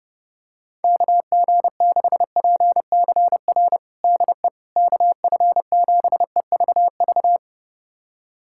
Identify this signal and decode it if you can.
Morse code, heard as a keyed tone, about 30 wpm, 700 Hz. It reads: KG6PCR DE KF7EVV
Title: USA Callsign DE Callsign - 6 Characters - 30wpm